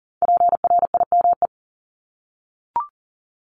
Morse code, heard as a keyed tone, about 38 wpm, 700 Hz. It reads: PRIME
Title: Multi-letter Phonemes 40wpm